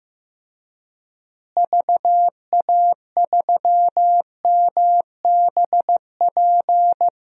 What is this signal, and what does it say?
Morse code, heard as a keyed tone, about 16 wpm, 700 Hz. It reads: VA3MBP
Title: DXCC-100 Call Signs - 6 Characters - Repeated 2x - 15wpm